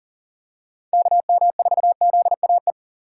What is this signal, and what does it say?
Morse code, heard as a keyed tone, about 40 wpm, 700 Hz. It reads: KM4ZAE
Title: USA Call Signs - 6 Characters - 4x Character Spacing - 40wpm